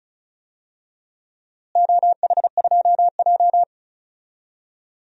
Morse code, heard as a keyed tone, about 36 wpm, 700 Hz. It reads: OH2J
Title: DXCC-100 Call Signs - 4 Characters - Repeated 2x - 35wpm